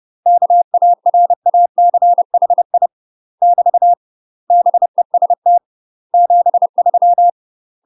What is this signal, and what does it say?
Morse code, heard as a keyed tone, about 30 wpm, 700 Hz. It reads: KARACHI <BT> BEST 73
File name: UK MASTER QSOs TRAINING / 30WPM / UK MASTER QSOs 0022